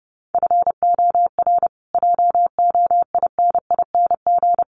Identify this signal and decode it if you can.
Morse code, heard as a keyed tone, about 30 wpm, 700 Hz. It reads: FOR JOINING
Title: Sets of 5 Words Encore 30wpm